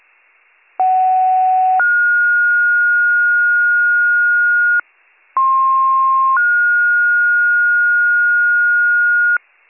Two-tone page, heard as a one-second tone, then a three-second tone, about 750 then 1430 Hz, about 1060 then 1430 Hz. It is heard from 0.8 to 9.4 s.